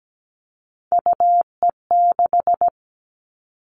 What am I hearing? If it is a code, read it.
Morse code, heard as a keyed tone, about 17 wpm, 700 Hz. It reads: UE6
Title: All Cut Numbers - 3-Digit 17wpm